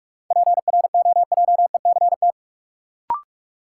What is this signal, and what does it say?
Morse code, heard as a keyed tone, about 46 wpm, 700 Hz. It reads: PROJECT
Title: Sentences from Top 1400 Words - New Word Drill Alt - 45wpm